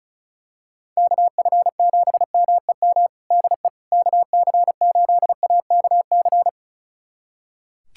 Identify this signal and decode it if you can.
Morse code, heard as a keyed tone, about 36 wpm, 700 Hz. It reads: KF7MEM DE KC8AKC
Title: USA Callsign DE Callsign - 6 Characters - 35wpm